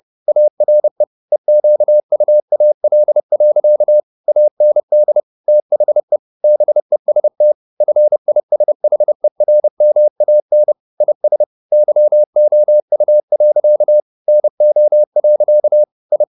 Morse code, heard as a keyed tone, about 30 wpm, 600 Hz. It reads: ARE EQUAL. AND THE BEST FISHERMAN IS YOU. NO. I
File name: the old man and the sea / the old man and the sea 1 30wpm